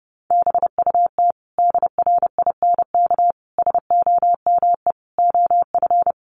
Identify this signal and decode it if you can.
Morse code, heard as a keyed tone, about 30 wpm, 700 Hz. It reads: BUT DRINK SOME OF
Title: Sentences from Top 1500 Words 30wpm